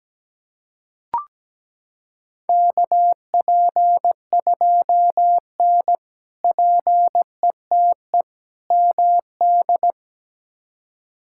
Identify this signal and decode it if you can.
Morse code, heard as a keyed tone, about 17 wpm, 700 Hz. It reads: KP2N PETE MD
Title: K1USN SST QSO Practice - 17wpm